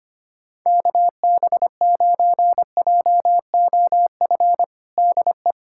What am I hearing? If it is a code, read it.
Morse code, heard as a keyed tone, about 25 wpm, 700 Hz. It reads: KB9JOF DE
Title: USA Callsign DE Callsign 25wpm